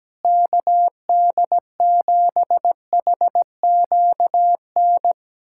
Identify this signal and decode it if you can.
Morse code, heard as a keyed tone, about 17 wpm, 700 Hz. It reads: KD7HQN
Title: USA Callsign DE Callsign 17wpm